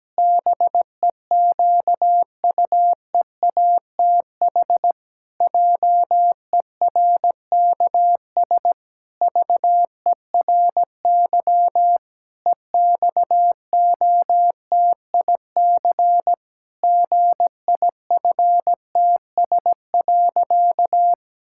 Morse code, heard as a keyed tone, about 17 wpm, 700 Hz. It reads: BEQUEATH JERKS VERY EXOTIC GIFTS.
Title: Pangram Challenge v2 - 17wpm